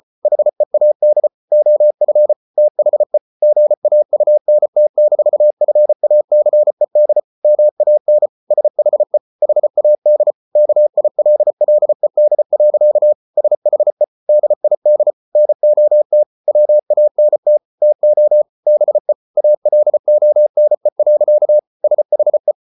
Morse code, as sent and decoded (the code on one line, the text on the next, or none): .... .- -.. / --- ..-. / - .... / --. .- ..- -. - -....- ..-. .- -.-. -.. / -- .- -. / ... .... / .... .- -.. / -.- .. .-.. .-.. -.. .-.-.- / ... .... / -.. .. -.. / -. --- - / .-- .- -. - / - --- / -... / .- .-.. --- -. .-.-.- / ... .... .
HEAD OF THE GAUNT-FACED MAN SHE HAD KILLED. SHE DID NOT WANT TO BE ALONE. SHE